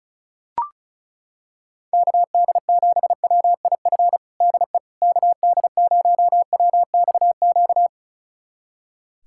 Morse code, heard as a keyed tone, about 36 wpm, 700 Hz. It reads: KD7WIF DE KD0WXQ
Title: USA Callsign DE Callsign - 6 Characters - 35wpm